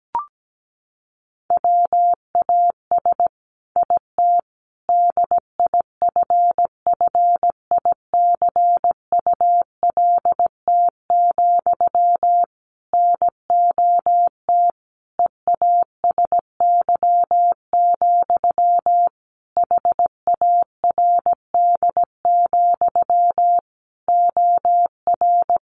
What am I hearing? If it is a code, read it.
Morse code, heard as a keyed tone, about 17 wpm, 700 Hz. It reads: WAS IT DIFFICULT, NOT EASY, HARD, OR
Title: Sentences from Top 1200 Words 17wpm